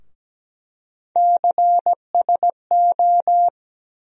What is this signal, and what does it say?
Morse code, heard as a keyed tone, about 17 wpm, 700 Hz. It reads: CSO
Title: USA Call Sign Postfixes - 3 Letters 17wpm